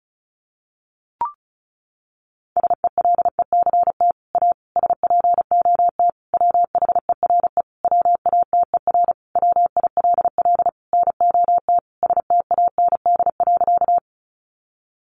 Morse code, heard as a keyed tone, about 34 wpm, 700 Hz. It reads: SELECT A SPOT WHERE WATER WILL NOT STAND.
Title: Sentences from Top 1300 Words 35wpm